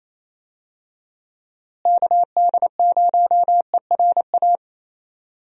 Morse code, heard as a keyed tone, about 28 wpm, 700 Hz. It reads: KD0ERA
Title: USA Call Signs - 6 Characters - Repeated 2x - 28wpm